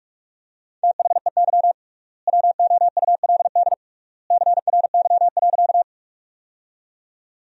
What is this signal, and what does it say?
Morse code, heard as a keyed tone, about 42 wpm, 700 Hz. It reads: THEY WOULD CRY.
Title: 3x Word-Spacing - Sentences from Top 500 Words - 45wpm